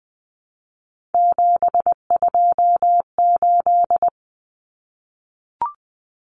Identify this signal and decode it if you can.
Morse code, heard as a keyed tone, about 20 wpm, 700 Hz. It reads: 728
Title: Common Cut Numbers - 3-Digit 20wpm